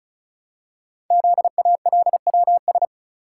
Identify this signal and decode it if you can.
Morse code, heard as a keyed tone, about 36 wpm, 700 Hz. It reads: ZALWS
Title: Code Groups - 5 Letters - 35wpm